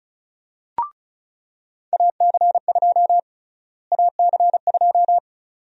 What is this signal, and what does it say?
Morse code, heard as a keyed tone, about 36 wpm, 700 Hz. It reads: AC2 AC2
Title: USA Call Sign Prefixes - Repeated 2x - 35wpm